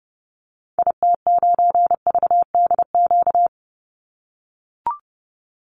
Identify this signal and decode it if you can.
Morse code, heard as a keyed tone, about 30 wpm, 700 Hz. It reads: IT9VDQ
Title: Call Signs - 30wpm